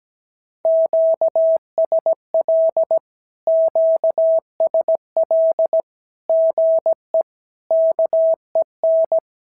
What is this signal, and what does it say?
Morse code, heard as a keyed tone, about 17 wpm, 650 Hz. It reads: QSL QSL GE KEN
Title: SOTA QSO Practice 17wpm